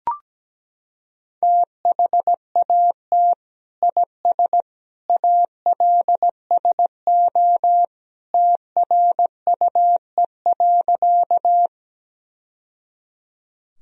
Morse code, heard as a keyed tone, about 17 wpm, 700 Hz. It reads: THAT IS ALSO TRUE.
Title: Sentences from Top 1000 Words - Review Part 1 - 17wpm